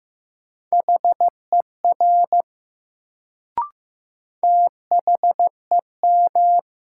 Morse code, heard as a keyed tone, about 15 wpm, 700 Hz. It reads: HER THEM
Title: Sentences from Top 100 Words - New Word Drill - 15wpm